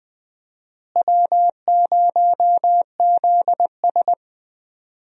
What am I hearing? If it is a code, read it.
Morse code, heard as a keyed tone, about 20 wpm, 700 Hz. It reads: W0ZS
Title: USA Call Signs - 4 Characters - Repeated 3x - 20wpm